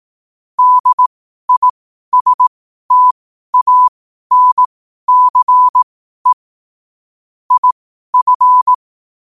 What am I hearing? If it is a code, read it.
Morse code, heard as a keyed tone, about 18 wpm, spaced as for 12 wpm, 1000 Hz. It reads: DISTANCE IF
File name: qotdfiles / 12wpm